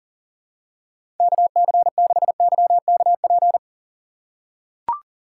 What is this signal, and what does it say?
Morse code, heard as a keyed tone, about 40 wpm, 700 Hz. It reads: KC6YKP
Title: USA Call Signs - 2x Character Spacing - 40wpm